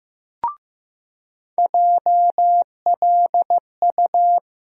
Morse code, heard as a keyed tone, about 15 wpm, 700 Hz. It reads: JLU
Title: USA Call Sign Postfixes - 3 Letters 15wpm